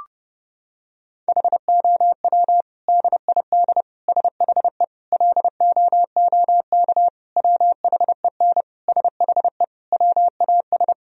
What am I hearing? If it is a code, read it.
Morse code, heard as a keyed tone, about 32 wpm, 700 Hz. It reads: HOW DID SHE LOOK WHEN SHE WAS
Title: Sentences from Top 1000 Words - Review Part 3 - 30wpm